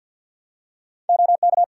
Morse code, heard as a keyed tone, about 46 wpm, 700 Hz. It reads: OK
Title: K1USN SST QSO Practice 50wpm